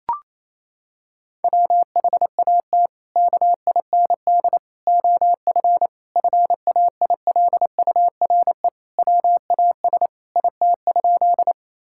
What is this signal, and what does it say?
Morse code, heard as a keyed tone, about 28 wpm, 700 Hz. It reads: WHAT KIND OF FAILURE WAS IT?
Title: Sentences from Top 1000 Words - Review Part 6 - 28wpm